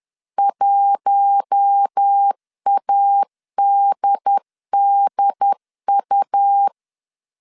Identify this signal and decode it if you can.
Morse code, heard as a keyed tone, about 10 wpm, 800 Hz. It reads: 1ADDU